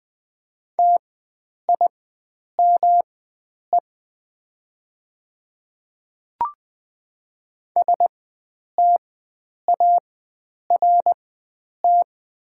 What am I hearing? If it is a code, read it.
Morse code, heard as a keyed tone, about 20 wpm, spaced as for 9 wpm, 700 Hz. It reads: TIME START
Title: 4x Character Spacing - Top 300 Words - 20wpm